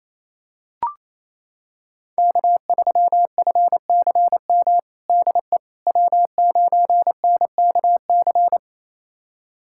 Morse code, heard as a keyed tone, about 28 wpm, 700 Hz. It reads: K3FCM DE W9NKC
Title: USA Callsign DE Callsign - 5 Characters - 28wpm